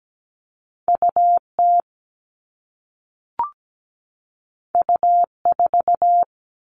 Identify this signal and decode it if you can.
Morse code, heard as a keyed tone, about 17 wpm, 700 Hz. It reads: UT U4
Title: All Cut Numbers - 2-Digit 17wpm